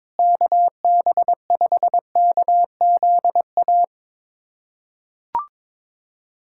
Morse code, heard as a keyed tone, about 22 wpm, 700 Hz. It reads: KB5KZA
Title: USA Callsign DE Callsign - 6 Characters - 22wpm